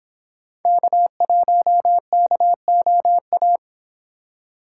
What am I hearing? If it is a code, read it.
Morse code, heard as a keyed tone, about 26 wpm, 700 Hz. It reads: K1KOA